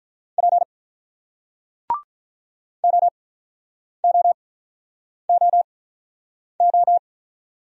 Morse code, heard as a keyed tone, about 47 wpm, spaced as for 9 wpm, 700 Hz. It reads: P OOOO